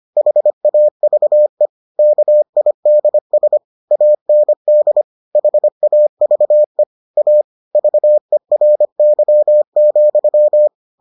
Morse code, heard as a keyed tone, about 25 wpm, 600 Hz. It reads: HAVE KIDS AND HAVE A VERY,